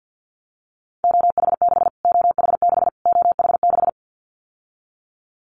Morse code, heard as a keyed tone, about 50 wpm, 700 Hz. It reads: OH6 OH6 OH6
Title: Callsign Prefixes for Top 25 DXCC - Repeated 3x - 50wpm